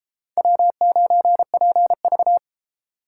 Morse code, heard as a keyed tone, about 30 wpm, 700 Hz. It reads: W9PV